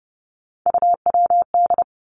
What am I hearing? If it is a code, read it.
Morse code, heard as a keyed tone, about 30 wpm, 700 Hz. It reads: UWD
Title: Three Challenging Letter-Number Combinations 30wpm